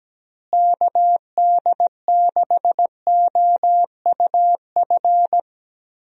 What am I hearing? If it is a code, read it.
Morse code, heard as a keyed tone, about 17 wpm, 700 Hz. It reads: KD6OUF
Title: USA Call Signs - 6 Characters - Repeated 3x - 17wpm